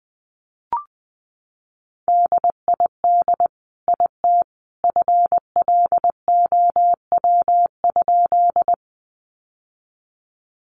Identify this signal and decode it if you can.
Morse code, heard as a keyed tone, about 20 wpm, 700 Hz. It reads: DID IT FLOW?
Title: Sentences from Top 1200 Words 20wpm